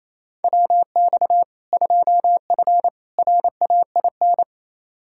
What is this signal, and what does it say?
Morse code, heard as a keyed tone, about 28 wpm, 700 Hz. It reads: WX 2F RAIN
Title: K1USN SST QSO Practice 28wpm